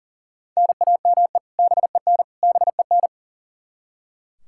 Morse code, heard as a keyed tone, about 40 wpm, 700 Hz. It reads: NAME BEN BEN
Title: Practice QSO - Exchange 1 - Beginning - 40wpm